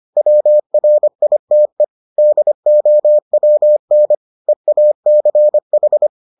Morse code, heard as a keyed tone, about 25 wpm, 600 Hz. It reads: WRITE DOWN EACH